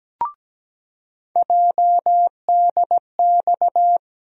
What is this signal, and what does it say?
Morse code, heard as a keyed tone, about 17 wpm, 700 Hz. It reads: JDX
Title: USA Call Sign Postfixes - 3 Letters 17wpm